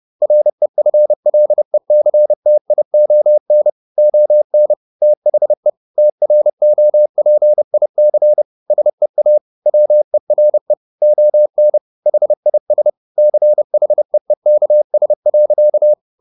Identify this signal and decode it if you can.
Morse code, heard as a keyed tone, about 30 wpm, 600 Hz. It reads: REFLECTION ON THE TROPIC SEA WERE ON HIS CHEEKS.
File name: the old man and the sea / the old man and the sea 1 30wpm